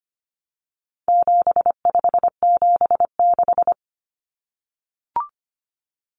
Morse code, heard as a keyed tone, about 25 wpm, 700 Hz. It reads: 7576
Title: Common Cut Numbers - 4-Digit 25wpm